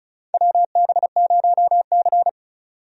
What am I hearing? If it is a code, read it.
Morse code, heard as a keyed tone, about 36 wpm, 700 Hz. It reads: WB0C
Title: USA Call Signs - 4 Characters - 35wpm